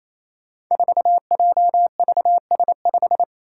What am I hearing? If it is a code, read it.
Morse code, heard as a keyed tone, about 29 wpm, 700 Hz. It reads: 4JVS5
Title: Five Challenging Letter-Number Combinations 28wpm